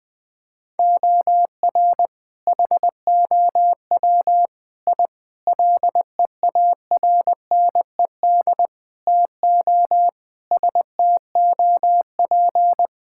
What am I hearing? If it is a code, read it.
Morse code, heard as a keyed tone, about 20 wpm, 700 Hz. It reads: OR HOW I LEARNED TO STOP
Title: Sentences with 8-Word Phrases - 20wpm